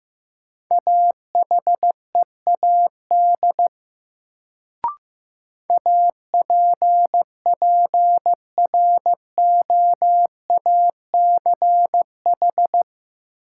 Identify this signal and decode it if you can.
Morse code, heard as a keyed tone, about 15 wpm, 700 Hz. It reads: AHEAD APPROACH
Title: Sentences from Top 1000 Words - Review Part 6 - New Word Drill - 15wpm